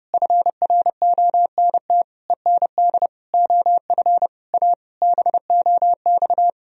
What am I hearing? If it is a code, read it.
Morse code, heard as a keyed tone, about 32 wpm, 700 Hz. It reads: FRONT END OF A BOX
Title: Sentences from Top 1000 Words - Review Part 2 - 30wpm